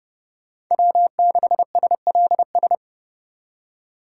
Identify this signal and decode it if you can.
Morse code, heard as a keyed tone, about 36 wpm, 700 Hz. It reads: W6SLS
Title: Speed Racing - USA 1x3 Call Signs 30wpm